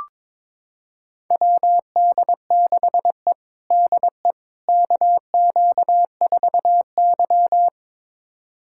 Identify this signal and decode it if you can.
Morse code, heard as a keyed tone, about 21 wpm, 700 Hz. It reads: WD6E DE KQ4Y
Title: USA Callsign DE Callsign - 4 Characters - 22wpm